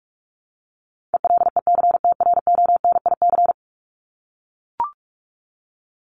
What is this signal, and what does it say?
Morse code, heard as a keyed tone, about 44 wpm, 700 Hz. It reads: ELECTRONIC
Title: Top 1001-1250 Words 45wpm